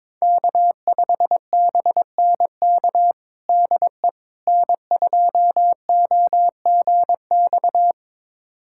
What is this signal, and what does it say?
Morse code, heard as a keyed tone, about 22 wpm, 700 Hz. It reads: K5BNK DE N2OGX
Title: USA Callsign DE Callsign - 5 Characters - 22wpm